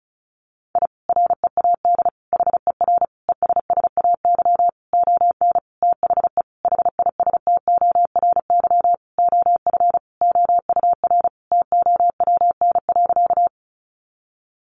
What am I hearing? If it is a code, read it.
Morse code, heard as a keyed tone, about 34 wpm, 700 Hz. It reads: I READ HER ESSAY ON THE HISTORY OF OUR TOWN.
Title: Sentences from Top 1500 Words 35wpm